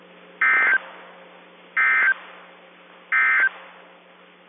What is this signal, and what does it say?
Emergency Alert System (SAME) tones, heard as harsh buzzing bursts. It is heard from 0.3 to 3.6 s.
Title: KIH24 NOAA Weather Radio